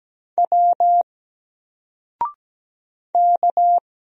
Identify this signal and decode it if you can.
Morse code, heard as a keyed tone, about 17 wpm, 700 Hz. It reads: W K
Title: Single Letter-Number Alt 17wpm